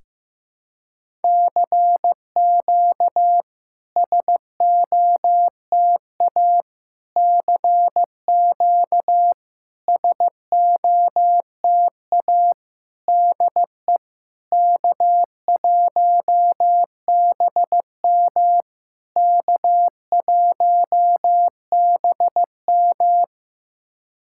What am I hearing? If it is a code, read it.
Morse code, heard as a keyed tone, about 15 wpm, 700 Hz. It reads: CQ SOTA CQ SOTA DE K1BM K1BM
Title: SOTA QSO Practice 15wpm